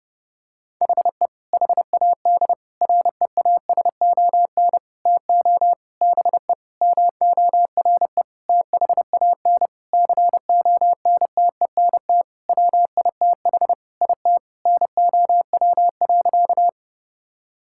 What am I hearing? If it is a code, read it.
Morse code, heard as a keyed tone, about 28 wpm, 700 Hz. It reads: HE HAD REASON TO BE MORE THAN CONTENT WITH IT NOW.
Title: Sentences from Top 1000 Words - Review Part 6 - 30wpm